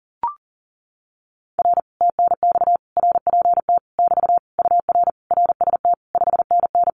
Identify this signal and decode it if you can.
Morse code, heard as a keyed tone, about 37 wpm, 700 Hz. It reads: R TNX RPT <BT> UR RST 5NN
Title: Common QSO Phrases Encore 40wpm